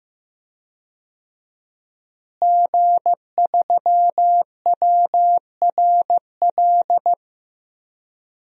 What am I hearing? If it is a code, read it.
Morse code, heard as a keyed tone, about 15 wpm, 700 Hz. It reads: G3WRL
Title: DXCC-100 Call Signs - 5 Characters - Repeated 2x - 15wpm